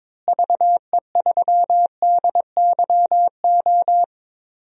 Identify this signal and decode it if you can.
Morse code, heard as a keyed tone, about 21 wpm, 700 Hz. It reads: VE3DYO
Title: DXCC-100 Call Signs - 6 Characters - Repeated 3x - 22wpm